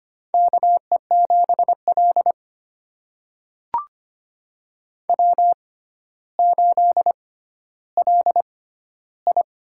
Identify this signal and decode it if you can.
Morse code, heard as a keyed tone, about 25 wpm, spaced as for 13 wpm, 700 Hz. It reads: KE7L W8LI
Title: USA Call Signs - 4 Characters - 6x Character Spacing - 25wpm